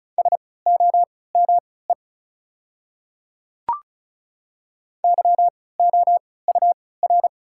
Morse code, heard as a keyed tone, about 36 wpm, spaced as for 20 wpm, 690 Hz. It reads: SOME YOUR
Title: 3x Character Spacing - Top 100 Words - 35wpm